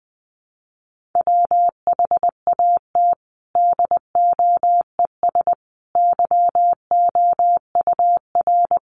Morse code, heard as a keyed tone, about 20 wpm, 700 Hz. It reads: WHAT DOES YOUR
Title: Sentences from Top 1500 Words 20wpm